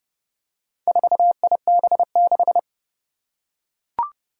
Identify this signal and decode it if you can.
Morse code, heard as a keyed tone, about 30 wpm, 700 Hz. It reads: 4IB6
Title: Four Challenging Letter-Number Combinations 30wpm